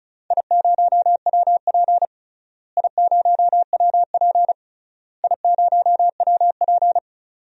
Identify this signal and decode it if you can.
Morse code, heard as a keyed tone, about 36 wpm, 700 Hz. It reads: I0WP I0WP I0WP
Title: DXCC-100 Call Signs - 4 Characters - Repeated 3x - 35wpm